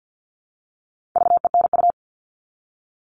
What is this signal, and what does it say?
Morse code, heard as a keyed tone, about 50 wpm, 700 Hz. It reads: VENU
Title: All Cut Numbers - 4-Digit 50wpm